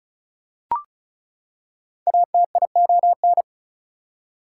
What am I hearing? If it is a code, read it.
Morse code, heard as a keyed tone, about 36 wpm, 700 Hz. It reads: ATION
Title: Prefixes and Suffixes 35wpm